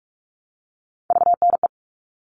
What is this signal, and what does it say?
Morse code, heard as a keyed tone, about 44 wpm, 700 Hz. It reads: VNE
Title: All Cut Numbers - 3-Digit 45wpm